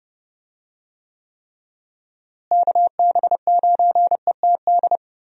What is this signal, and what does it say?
Morse code, heard as a keyed tone, about 30 wpm, 700 Hz. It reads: KB9ETD
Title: USA Call Signs - 6 Characters - Repeated 3x - 30wpm